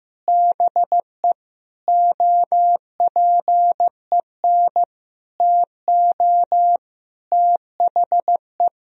Morse code, heard as a keyed tone, about 15 wpm, 700 Hz. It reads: BE OPEN TO THE
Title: Sentences from Top 1000 Words - Review Part 3 - 15wpm